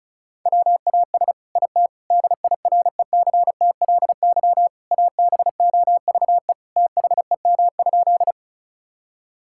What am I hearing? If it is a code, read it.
Morse code, heard as a keyed tone, about 36 wpm, 700 Hz. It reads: WAS IT DIRECTLY ABOVE THEM?
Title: Sentences from Top 1000 Words - Review Part 5 - 35wpm